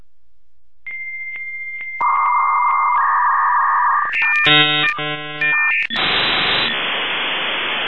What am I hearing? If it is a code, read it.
Dial-up modem handshake.